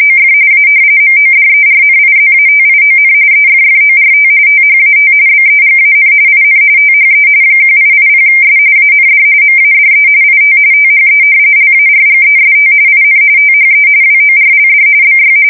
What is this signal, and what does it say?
RTTY, about 45.45 baud. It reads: OF A MODERNIZATION EFFORT THAT BEGAN IN THE 1880S AND MADE IT THE LARGEST IN THE WORLD BY
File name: fsk170 / usnavy1